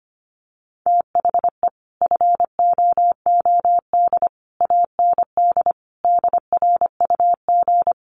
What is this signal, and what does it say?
Morse code, heard as a keyed tone, about 25 wpm, 700 Hz. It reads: THE FOOD AND DRUG
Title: Sets of 5 Words 25wpm